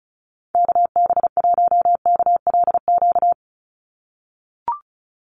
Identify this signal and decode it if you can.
Morse code, heard as a keyed tone, about 34 wpm, 700 Hz. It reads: KB1KLQ
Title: USA Call Signs 35wpm